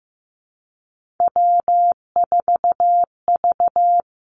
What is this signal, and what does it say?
Morse code, heard as a keyed tone, about 15 wpm, 700 Hz. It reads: W4V
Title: Three Challenging Letter-Number Combinations 15wpm